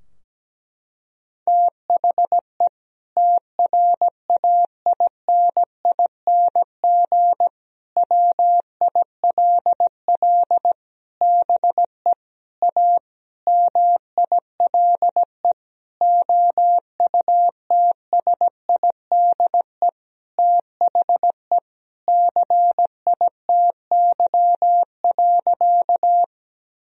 Morse code, heard as a keyed tone, about 17 wpm, 700 Hz. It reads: THE TRAINING WILL BE A MILE OUTSIDE THE CITY.
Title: Sentences from Top 1000 Words - Review Part 4 - 17wpm